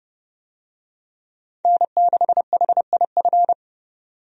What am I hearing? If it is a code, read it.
Morse code, heard as a keyed tone, about 30 wpm, 700 Hz. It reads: N6HIF
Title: USA Call Signs - 5 Characters - 30wpm